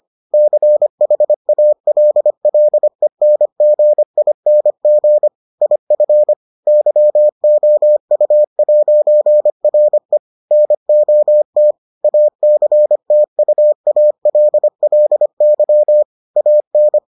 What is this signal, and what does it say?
Morse code, heard as a keyed tone, about 25 wpm, 600 Hz. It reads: CHALLENGING IF YOU'RE NOT ACTUALLY AN